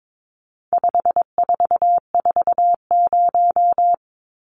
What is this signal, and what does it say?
Morse code, heard as a keyed tone, about 22 wpm, 700 Hz. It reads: 5440
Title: Numbers - 4-Digit 22wpm